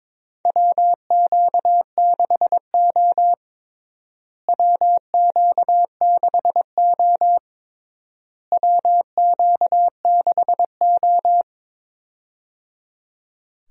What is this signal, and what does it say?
Morse code, heard as a keyed tone, about 22 wpm, 700 Hz. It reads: WQ6O WQ6O WQ6O
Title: USA Call Signs - 4 Characters - Repeated 3x - 22wpm